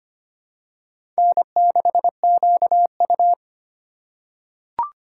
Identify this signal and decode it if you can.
Morse code, heard as a keyed tone, about 25 wpm, 700 Hz. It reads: N6QU
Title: USA Call Signs - 4 Characters - 3x Character Spacing - 25wpm